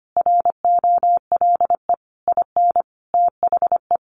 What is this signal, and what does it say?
Morse code, heard as a keyed tone, about 25 wpm, 700 Hz. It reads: ROLE IN THE
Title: Sets of 5 Words Encore2 25wpm